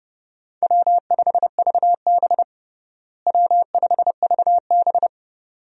Morse code, heard as a keyed tone, about 30 wpm, 700 Hz. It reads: W5VB W5VB
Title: USA Call Signs - 4 Characters - Repeated 2x - 30wpm